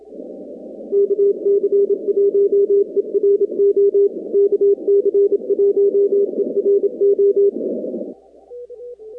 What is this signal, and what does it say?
Morse code, heard as a keyed tone, about 27 wpm, 400 Hz. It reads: KC1ERO KC1ERO
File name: contests / 2019-cw-wpx-kt5j-k5pi- -2330 0-45